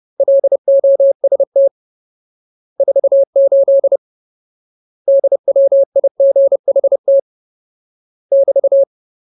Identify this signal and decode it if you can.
Morse code, heard as a keyed tone, about 30 wpm, 550 Hz. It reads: LOST 48 DWIGHT <BT>
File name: Ultimate Word List / 30WPM Ultimate